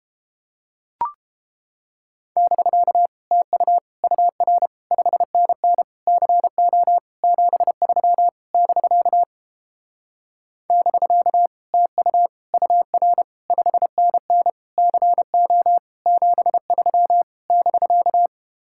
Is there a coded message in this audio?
Yes: Morse code from 2.4 to 18.3 s, heard as a keyed tone, about 31 wpm, 700 Hz. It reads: <BK> TU UR 5NN CO 73 <BK> <BK> TU UR 5NN CO 73 <BK>